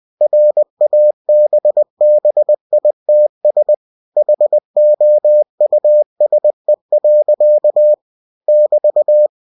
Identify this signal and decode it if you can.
Morse code, heard as a keyed tone, about 20 wpm, 600 Hz. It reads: RABBITS HOUSE. <BT>